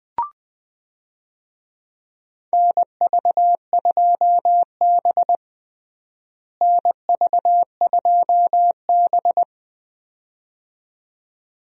Morse code, heard as a keyed tone, about 21 wpm, 700 Hz. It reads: NV2B NV2B
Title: USA Call Signs - 4 Characters - Repeated 2x - 20wpm